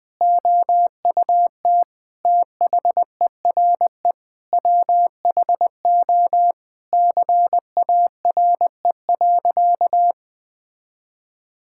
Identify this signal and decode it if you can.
Morse code, heard as a keyed tone, about 20 wpm, 700 Hz. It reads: OUT THERE WHO CARE.
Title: Sentences with 9-Word Phrases - 20wpm